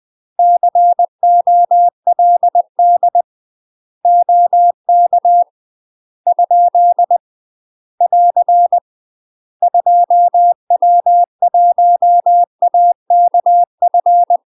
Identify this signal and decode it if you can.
Morse code, heard as a keyed tone, about 20 wpm, 700 Hz. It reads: COLD OK ? <AR> 2W1AKF
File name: UK MASTER QSOs TRAINING / 20WPM / UK MASTER QSOs 0010